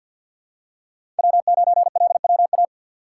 Morse code, heard as a keyed tone, about 50 wpm, 700 Hz. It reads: W9LWA
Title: USA Call Signs - 5 Characters - 3x Character Spacing - 50wpm